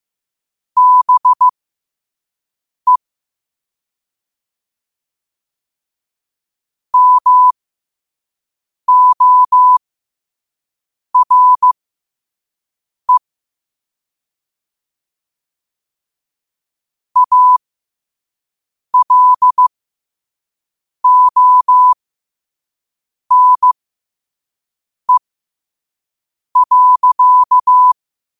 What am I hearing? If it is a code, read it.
Morse code, heard as a keyed tone, about 15 wpm, spaced as for 5 wpm, 1000 Hz. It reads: BE MORE ALONE.
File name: qotdfiles / 5wpm